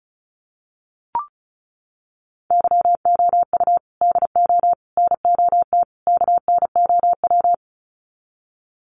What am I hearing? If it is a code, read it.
Morse code, heard as a keyed tone, about 34 wpm, 700 Hz. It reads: YOU DO NOT KNOW
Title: Sets of 4 Words 35wpm